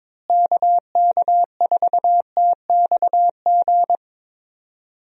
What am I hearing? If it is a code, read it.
Morse code, heard as a keyed tone, about 22 wpm, 700 Hz. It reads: KK4TXG
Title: USA Call Signs - 6 Characters - Repeated 2x - 22wpm